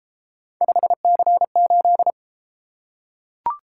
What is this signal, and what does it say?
Morse code, heard as a keyed tone, about 32 wpm, 700 Hz. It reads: 5C8